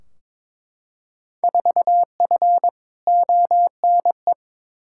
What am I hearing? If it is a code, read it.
Morse code, heard as a keyed tone, about 22 wpm, 700 Hz. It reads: 4F ONE
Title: Field Day Exchanges Encore - 22wpm